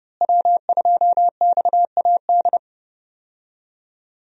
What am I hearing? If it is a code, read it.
Morse code, heard as a keyed tone, about 30 wpm, 700 Hz. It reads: W2XAD
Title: USA Call Signs - 5 Characters - Repeated 3x - 30wpm